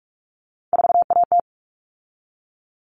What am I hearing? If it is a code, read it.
Morse code, heard as a keyed tone, about 44 wpm, 700 Hz. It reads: 4AT
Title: All Cut Numbers - 3-Digit 45wpm